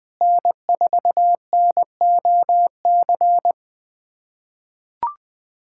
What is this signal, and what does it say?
Morse code, heard as a keyed tone, about 20 wpm, 690 Hz. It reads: N4NOC
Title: SOTA QSO Practice 20wpm